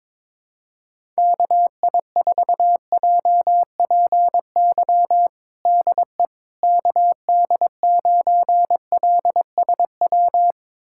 Morse code, heard as a keyed tone, about 22 wpm, 700 Hz. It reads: KI4JPY DE KD9LSW
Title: USA Callsign DE Callsign 22wpm